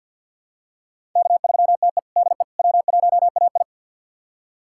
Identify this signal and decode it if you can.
Morse code, heard as a keyed tone, about 48 wpm, 700 Hz. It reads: K3TE DE W1AI
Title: USA Callsign DE Callsign - 4 Characters - 50wpm